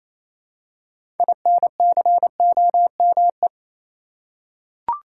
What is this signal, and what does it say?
Morse code, heard as a keyed tone, about 28 wpm, 700 Hz. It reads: INCOME
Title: Six Letter Words - 28wpm